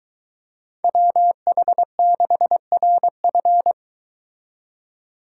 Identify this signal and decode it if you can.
Morse code, heard as a keyed tone, about 23 wpm, 700 Hz. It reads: WH6RF